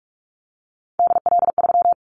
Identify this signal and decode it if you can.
Morse code, heard as a keyed tone, about 44 wpm, 700 Hz. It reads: DL3
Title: Callsign Prefixes for Top 50 DXCC - 45wpm